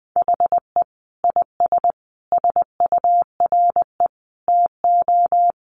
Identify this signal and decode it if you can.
Morse code, heard as a keyed tone, about 20 wpm, 700 Hz. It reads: HE IS SURE TO
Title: Sentences from Top 700 Words 20wpm